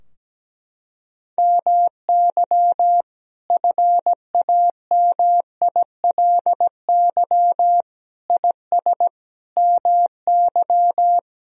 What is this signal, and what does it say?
Morse code, heard as a keyed tone, about 17 wpm, 700 Hz. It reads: MY FAMILY IS MY
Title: Sentences from Top 1000 Words - Review Part 4 - 17wpm